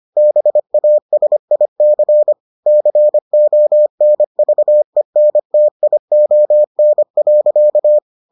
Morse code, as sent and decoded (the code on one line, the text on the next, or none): -... .- ... .. -.-. / -.-. --- -. ...- . -. - .. --- -. .-.-.-
BASIC CONVENTION.